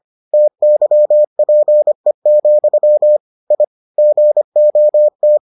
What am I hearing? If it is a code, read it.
Morse code, heard as a keyed tone, about 25 wpm, 600 Hz. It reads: TYPE, I GOT